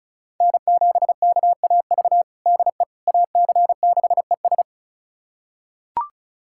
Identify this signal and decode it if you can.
Morse code, heard as a keyed tone, about 36 wpm, 700 Hz. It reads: N7KAV DE AC6ES
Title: USA Callsign DE Callsign - 5 Characters - 35wpm